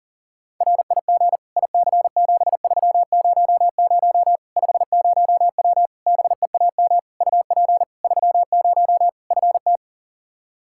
Morse code, heard as a keyed tone, about 40 wpm, 700 Hz. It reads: RIG IC7300 50W BEAM UP 30 FT
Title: Practice QSO - Exchange 2 - Info - 40wpm